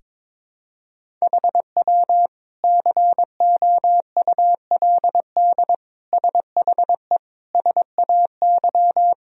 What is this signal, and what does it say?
Morse code, heard as a keyed tone, about 22 wpm, 700 Hz. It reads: HW COULD SHE SAY
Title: Sentences from Top 200 Words and Abbreviations 22wpm